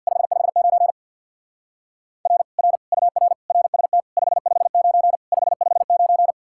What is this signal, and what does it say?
Morse code, heard as a keyed tone, about 61 wpm, 700 Hz. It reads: 559 R R UR RST 559 559
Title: Speed-Racing Common QSO Phrases - 50wpm